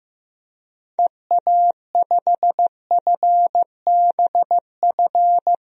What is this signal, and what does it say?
Morse code, heard as a keyed tone, about 15 wpm, 700 Hz. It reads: EA5FBF
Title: DXCC-100 Call Signs - 6 Characters - 15wpm